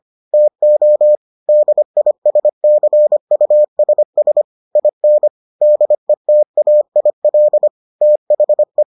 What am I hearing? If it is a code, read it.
Morse code, heard as a keyed tone, about 25 wpm, 600 Hz. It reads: TO DISCUSS IN DETAIL THE